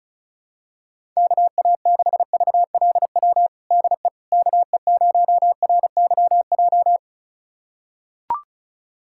Morse code, heard as a keyed tone, about 34 wpm, 700 Hz. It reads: KA6VLW DE KE0RYJ
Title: USA Callsign DE Callsign - 6 Characters - 35wpm